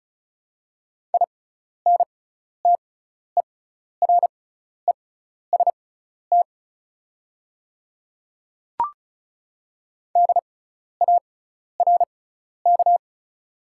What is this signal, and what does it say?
Morse code, heard as a keyed tone, about 34 wpm, spaced as for 12 wpm, 700 Hz. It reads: INTEREST DARK
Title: 6x Character Spacing - Top 500 Words - 35wpm